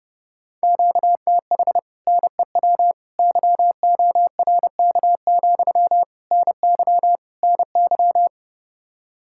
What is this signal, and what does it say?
Morse code, heard as a keyed tone, about 30 wpm, 700 Hz. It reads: QTH NEW YORK, NY NY